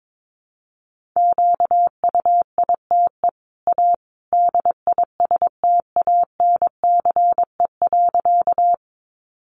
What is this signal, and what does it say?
Morse code, heard as a keyed tone, about 22 wpm, 700 Hz. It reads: QUITE A DISTANCE.
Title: Sentences from Top 1200 Words 22wpm